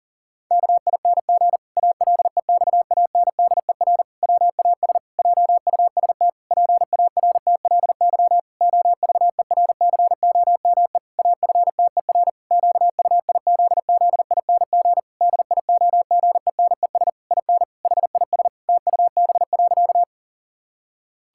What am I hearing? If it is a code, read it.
Morse code, heard as a keyed tone, about 37 wpm, 700 Hz. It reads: KING ALEXANDER WAS JUST PARTLY OVERCOME AFTER QUIZZING DIOGENES IN HIS TUB.
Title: Pangram Challenge v2 - 40wpm